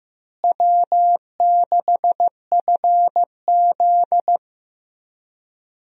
Morse code, heard as a keyed tone, about 15 wpm, 700 Hz. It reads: W6FZ
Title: USA Call Signs - 4 Characters - Repeated 2x - 15wpm